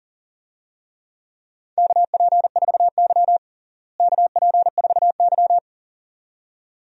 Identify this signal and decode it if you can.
Morse code, heard as a keyed tone, about 37 wpm, 700 Hz. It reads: KP4Y KP4Y
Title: DXCC-100 Call Signs - 4 Characters - Repeated 2x - 40wpm